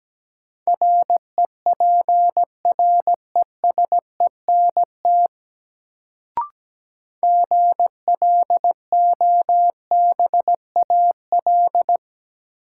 Morse code, heard as a keyed tone, about 17 wpm, 700 Hz. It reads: REPRESENT GLOBAL
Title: Sentences from Top 1200 Words - New Word Drill - 17wpm